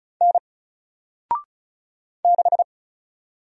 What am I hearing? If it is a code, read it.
Morse code, heard as a keyed tone, about 36 wpm, 700 Hz. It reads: N 6
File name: Single-Letter-Number-35wpm